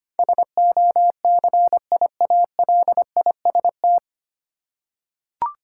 Morse code, heard as a keyed tone, about 25 wpm, 700 Hz. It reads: SOCIALIST
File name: Nine Letter Words - 25wpm